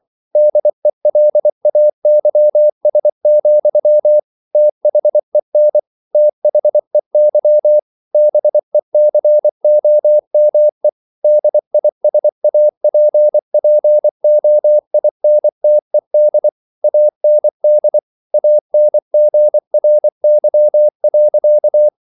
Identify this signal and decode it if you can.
Morse code, heard as a keyed tone, about 24 wpm, 600 Hz. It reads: DELAYS, THEN THEY BECOME DISAPPOINTED AND ANGRY.